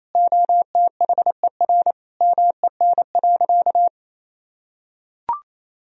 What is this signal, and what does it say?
Morse code, heard as a keyed tone, about 28 wpm, 700 Hz. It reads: OTHER MEN.
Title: Sentences from Top 1000 Words - Review Part 3 - 28wpm